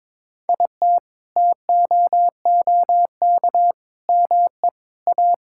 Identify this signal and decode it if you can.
Morse code, heard as a keyed tone, about 22 wpm, 700 Hz. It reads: IT TOOK ME A
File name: Sentences with 7-Word Phrases - 22wpm